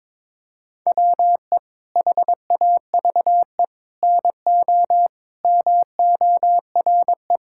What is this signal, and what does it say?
Morse code, heard as a keyed tone, about 22 wpm, 700 Hz. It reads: WE HAVE NO MORE
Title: Sentences from Top 1000 Words - Review Part 5 - 22wpm